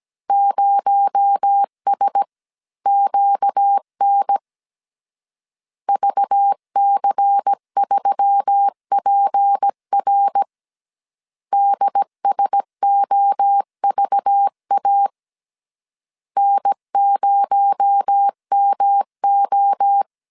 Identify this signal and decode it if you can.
Morse code, heard as a keyed tone, about 17 wpm, 800 Hz. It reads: 0SQN VC3PR DSOVA N0MO
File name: CW / DET / 1 6 1500C S A 16wpm 2635